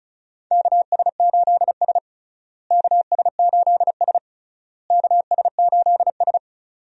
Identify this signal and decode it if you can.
Morse code, heard as a keyed tone, about 36 wpm, 700 Hz. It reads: KS8S KS8S KS8S
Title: USA Call Signs - 4 Characters - Repeated 3x - 35wpm